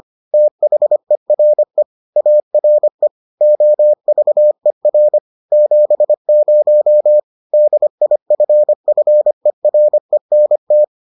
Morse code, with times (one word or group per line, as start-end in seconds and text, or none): 0.3-1.8 THERE
2.2-3.1 ARE
3.4-5.2 OVER
5.5-7.2 70
7.5-10.8 DIFFERENT